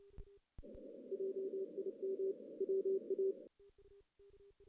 Morse code, heard as a keyed tone, about 30 wpm, 410 Hz. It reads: JIM WA
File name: 2016 JAN CW NAQP / 2016 Jan naqp k5ot - 0-6